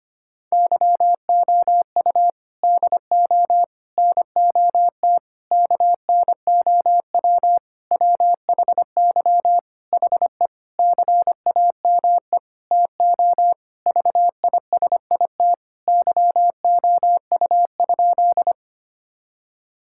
Morse code, heard as a keyed tone, about 25 wpm, 700 Hz. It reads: YOU DO NOT KNOW WHY HE CAME TO VISIT YOU?
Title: Sentences from Top 1000 Words - Review Part 6 - 25wpm